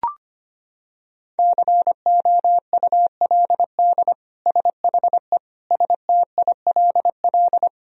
Morse code, heard as a keyed tone, about 25 wpm, 700 Hz. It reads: COULD SHE STILL
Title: Sentences from Top 1000 Words - Review Part 1 - 25wpm